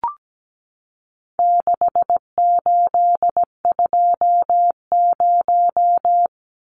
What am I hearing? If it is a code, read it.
Morse code, heard as a keyed tone, about 17 wpm, 700 Hz. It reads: 6820
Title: Numbers - 4-Digit 17wpm